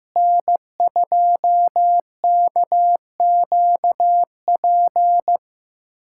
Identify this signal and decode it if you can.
Morse code, heard as a keyed tone, about 15 wpm, 700 Hz. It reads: N2KQP
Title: USA Callsign DE Callsign - 5 Characters - 15wpm